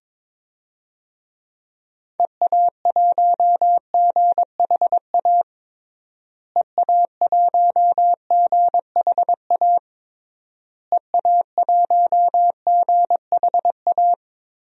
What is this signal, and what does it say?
Morse code, heard as a keyed tone, about 22 wpm, 700 Hz. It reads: EA1GHA EA1GHA EA1GHA
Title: DXCC-100 Call Signs - 6 Characters - Repeated 3x - 22wpm